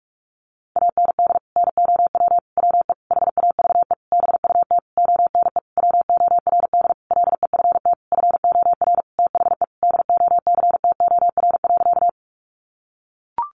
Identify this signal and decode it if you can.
Morse code, heard as a keyed tone, about 44 wpm, 700 Hz. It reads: AND NOW WE HAVE BUT ONE WORD LEFT FOR THE DOCTOR.
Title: Sentences from Top 1500 Words 45wpm